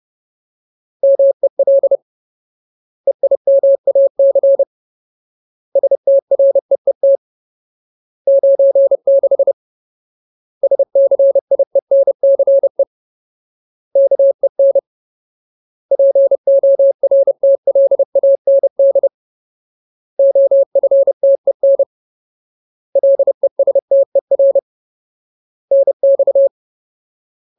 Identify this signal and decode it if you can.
Morse code, heard as a keyed tone, about 30 wpm, 550 Hz. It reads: MEL EIMAC STREET 96 SCIENCE KEN PORTLAND OFTEN LESTER NX
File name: Ultimate Word List / 30WPM Ultimate